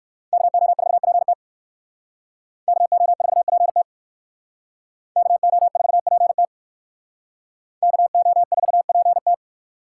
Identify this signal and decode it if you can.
Morse code, heard as a keyed tone, about 57 wpm, 700 Hz. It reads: KO4PT KO4PT KO4PT KO4PT